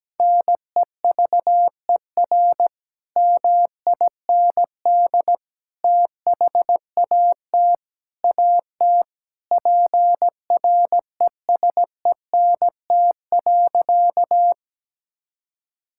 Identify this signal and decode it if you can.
Morse code, heard as a keyed tone, about 17 wpm, 700 Hz. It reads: NEVER MIND THAT AT PRESENT.
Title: Sentences from Top 1000 Words - Review Part 5 - 17wpm